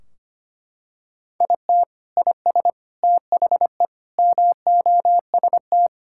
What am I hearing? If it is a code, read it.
Morse code, heard as a keyed tone, about 25 wpm, 700 Hz. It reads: IT IS THE MOST
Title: Sentences from Top 1600 Words 25wpm